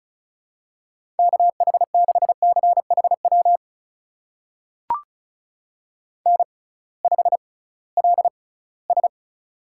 Morse code, heard as a keyed tone, about 36 wpm, spaced as for 20 wpm, 700 Hz. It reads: KH6CHW N5LS
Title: USA Call Signs - 6x Character Spacing - 35wpm